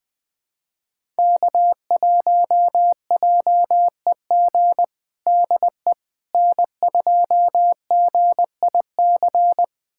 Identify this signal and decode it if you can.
Morse code, heard as a keyed tone, about 20 wpm, 700 Hz. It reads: K1JEG DE N2GIC
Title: USA Callsign DE Callsign 20wpm